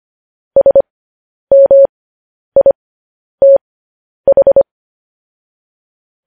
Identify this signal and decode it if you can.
Morse code, heard as a keyed tone, about 25 wpm, spaced as for 10 wpm, 550 Hz. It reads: SMITH
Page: news-headlines-in-morse-code-at-10-wpm